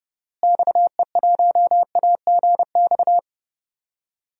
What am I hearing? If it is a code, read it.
Morse code, heard as a keyed tone, about 32 wpm, 700 Hz. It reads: XE1AGX
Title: DXCC-100 Call Signs - 6 Characters - Repeated 2x - 30wpm